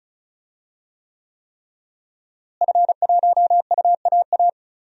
Morse code, heard as a keyed tone, about 34 wpm, 700 Hz. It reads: F1UAA
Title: DXCC-100 Call Signs - 5 Characters - Repeated 3x - 35wpm